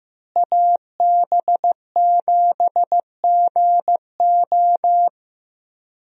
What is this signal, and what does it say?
Morse code, heard as a keyed tone, about 15 wpm, 700 Hz. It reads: AB7GO
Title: USA Call Signs - 5 Characters - Repeated 2x - 15wpm